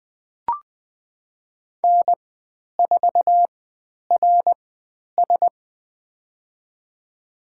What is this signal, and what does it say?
Morse code, heard as a keyed tone, about 20 wpm, spaced as for 10 wpm, 700 Hz. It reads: N4RS
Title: 102 Lesson 17g - Letter u - Call Signs 20wpm